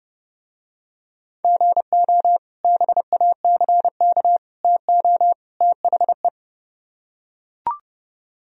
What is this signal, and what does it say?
Morse code, heard as a keyed tone, about 29 wpm, 700 Hz. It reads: GO BACK TO THE